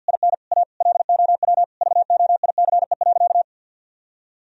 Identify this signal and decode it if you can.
Morse code, heard as a keyed tone, about 49 wpm, 700 Hz. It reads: IN A LOW VOICE.
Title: Sentences from Top 1000 Words - Review Part 1 - 50wpm